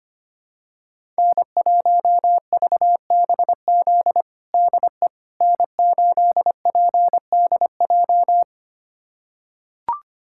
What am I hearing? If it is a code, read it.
Morse code, heard as a keyed tone, about 25 wpm, 700 Hz. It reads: N1VBZ DE N8PDJ
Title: USA Callsign DE Callsign - 5 Characters - 25wpm